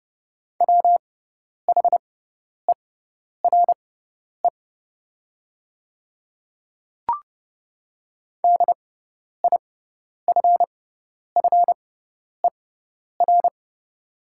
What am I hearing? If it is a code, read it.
Morse code, heard as a keyed tone, about 30 wpm, spaced as for 10 wpm, 700 Hz. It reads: WHERE DIFFER
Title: 6x Character Spacing - Top 500 Words - 30wpm